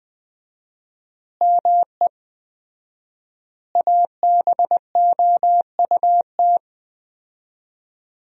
Morse code, heard as a keyed tone, about 20 wpm, 700 Hz. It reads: ME ABOUT
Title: 4x Word-Spacing - Sentences from Top 200 Words - 20wpm